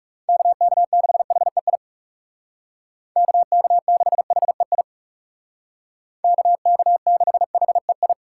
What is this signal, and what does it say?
Morse code, heard as a keyed tone, about 37 wpm, 700 Hz. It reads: KK6HEI KK6HEI KK6HEI
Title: Speed Racing - USA Call Signs 30wpm